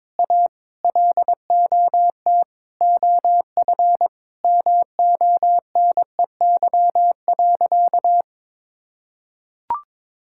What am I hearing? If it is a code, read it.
Morse code, heard as a keyed tone, about 22 wpm, 700 Hz. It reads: A LOT OF MONEY.
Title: Sentences from Top 1000 Words - Review Part 6 - 22wpm